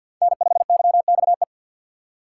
Morse code, heard as a keyed tone, about 58 wpm, 700 Hz. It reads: N5YXE
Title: Speed Racing - USA 1x3 Call Signs 50wpm